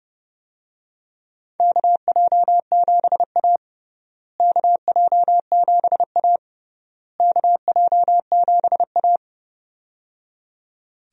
Morse code, heard as a keyed tone, about 30 wpm, 700 Hz. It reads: KJ7A KJ7A KJ7A
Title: USA Call Signs - 4 Characters - Repeated 3x - 30wpm